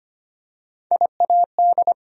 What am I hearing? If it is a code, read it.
Morse code, heard as a keyed tone, about 25 wpm, 700 Hz. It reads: IAD
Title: USA Call Sign Postfixes - 3 Letters 25wpm